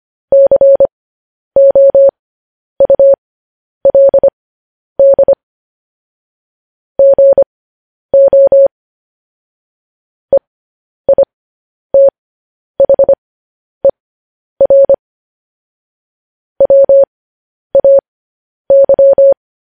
Morse code, heard as a keyed tone, about 25 wpm, spaced as for 10 wpm, 550 Hz. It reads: COULD GO EITHER WAY
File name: podcast / News headlines-1x-25x10